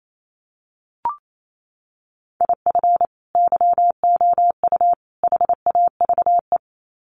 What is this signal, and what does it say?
Morse code, heard as a keyed tone, about 28 wpm, 700 Hz. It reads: IF YOU HAVE
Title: Sets of 3 Words 28wpm